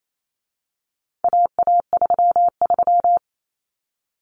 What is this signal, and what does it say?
Morse code, heard as a keyed tone, about 28 wpm, 700 Hz. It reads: AA33
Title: Common Cut Numbers - 4-Digit 28wpm